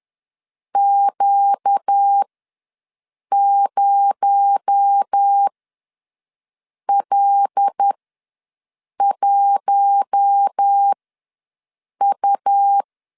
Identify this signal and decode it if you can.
Morse code, heard as a keyed tone, about 10 wpm, spaced as for 5 wpm, 800 Hz. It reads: Q0L1U
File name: LEZIONI / 6 PKX0 / 6 PKX0 ws 10wpm 4220